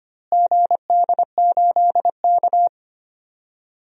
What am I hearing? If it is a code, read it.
Morse code, heard as a keyed tone, about 25 wpm, 700 Hz. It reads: GD8K
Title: DXCC-100 Call Signs - 4 Characters - Repeated 2x - 25wpm